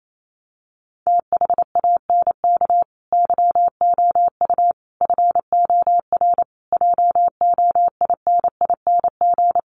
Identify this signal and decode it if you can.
Morse code, heard as a keyed tone, about 28 wpm, 700 Hz. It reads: THANK YOU FOR JOINING
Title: Sets of 5 Words 28wpm